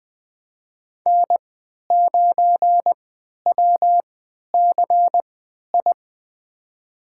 Morse code, heard as a keyed tone, about 20 wpm, spaced as for 11 wpm, 700 Hz. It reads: N9WCI
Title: USA Call Signs - 3x Character Spacing - 20wpm